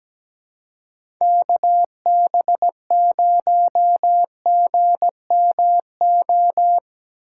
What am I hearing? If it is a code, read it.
Morse code, heard as a keyed tone, about 17 wpm, 700 Hz. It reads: KB0GMO
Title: USA Callsign DE Callsign 17wpm